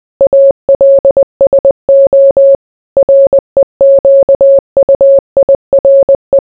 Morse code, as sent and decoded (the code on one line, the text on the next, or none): .- .-.. ... --- / .-. --.- ..- .. .-. .
ALSO REQUIRE